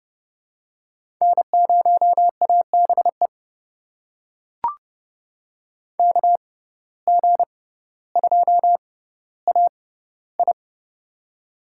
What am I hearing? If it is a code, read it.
Morse code, heard as a keyed tone, about 30 wpm, spaced as for 15 wpm, 700 Hz. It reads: N0ABE KG2AI
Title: USA Call Signs - 5 Characters - 6x Character Spacing - 30wpm